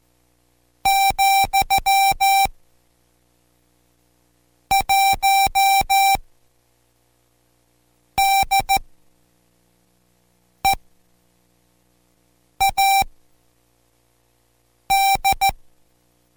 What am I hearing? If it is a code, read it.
Morse code, heard as a keyed tone, about 14 wpm, spaced as for 4 wpm, 780 Hz. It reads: , 1 DEAD